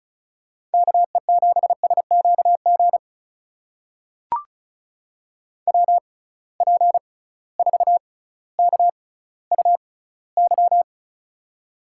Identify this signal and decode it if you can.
Morse code, heard as a keyed tone, about 36 wpm, spaced as for 18 wpm, 700 Hz. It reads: KE7SQG WP4KUY
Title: USA Call Signs - 6 Characters - 6x Character Spacing - 35wpm